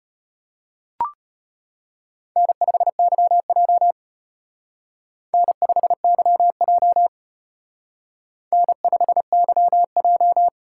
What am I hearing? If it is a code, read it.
Morse code, heard as a keyed tone, about 29 wpm, 700 Hz. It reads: N5YJ N5YJ N5YJ
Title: Speed Racing - USA 1x2 Call Signs 25wpm